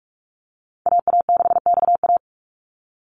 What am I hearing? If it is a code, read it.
Morse code, heard as a keyed tone, about 44 wpm, 700 Hz. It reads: AA6XA
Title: Call Signs Repeated 3x - 45wpm